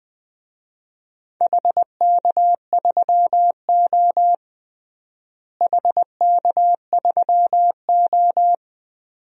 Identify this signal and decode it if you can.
Morse code, heard as a keyed tone, about 20 wpm, 700 Hz. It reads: HK3O HK3O
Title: DXCC-100 Call Signs - 4 Characters - Repeated 2x - 20wpm